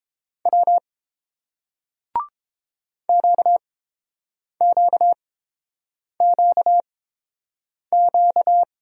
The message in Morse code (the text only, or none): W QQQQ